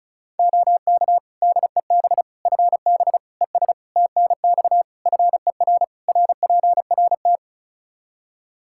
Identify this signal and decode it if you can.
Morse code, heard as a keyed tone, about 36 wpm, 700 Hz. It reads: OK DEB FB ES TNX FER RPRT
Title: Practice QSO - Exchange 2 - Info - 35wpm